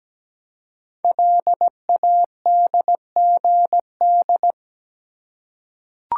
Morse code, heard as a keyed tone, about 17 wpm, 700 Hz. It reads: LADGD
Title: Code Groups - 5 Letters - 17wpm